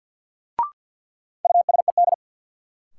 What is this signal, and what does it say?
Morse code, heard as a keyed tone, about 52 wpm, 700 Hz. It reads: USED
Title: Sentences from Top 600 Words - New Word Drill - 50wpm